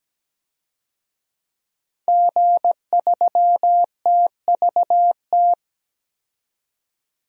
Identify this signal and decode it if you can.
Morse code, heard as a keyed tone, about 17 wpm, 700 Hz. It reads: G3TVT
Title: DXCC-100 Call Signs - 5 Characters - 17wpm